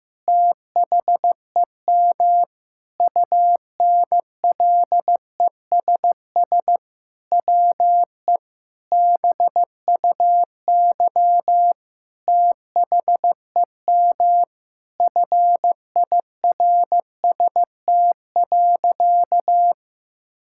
Morse code, heard as a keyed tone, about 16 wpm, 700 Hz. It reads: THEM UNLESS WE BUY THEM FIRST.
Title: Sentences from Top 1000 Words - Review Part 6 - 15wpm